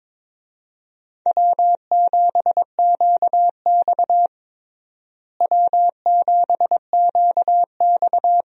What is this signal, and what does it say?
Morse code, heard as a keyed tone, about 22 wpm, 700 Hz. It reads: W7QX W7QX
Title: USA Call Signs - 4 Characters - Repeated 2x - 22wpm